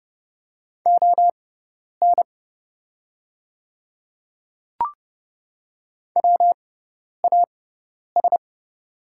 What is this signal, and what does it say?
Morse code, heard as a keyed tone, about 31 wpm, spaced as for 10 wpm, 700 Hz. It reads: ON WAS
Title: 6x Character Spacing - Top 100 Words - 30wpm